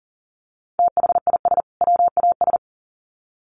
Morse code, heard as a keyed tone, about 40 wpm, 700 Hz. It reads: THIS WAS
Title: Sets of 2 Words 40wpm